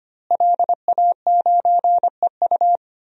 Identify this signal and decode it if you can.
Morse code, heard as a keyed tone, about 25 wpm, 700 Hz. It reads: LA9EU
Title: DXCC-100 Call Signs - 5 Characters - Repeated 2x - 25wpm